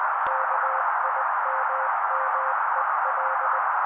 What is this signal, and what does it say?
Morse code, heard as a keyed tone, about 20 wpm, 530 Hz. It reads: KIMMEL
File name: daily / 18wpm noise